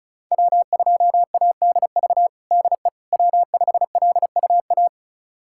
Morse code, heard as a keyed tone, about 34 wpm, 700 Hz. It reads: W2ADV DE W5LUA
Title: USA Callsign DE Callsign - 5 Characters - 35wpm